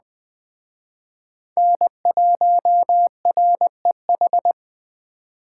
Morse code, heard as a keyed tone, about 21 wpm, 700 Hz. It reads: N1REH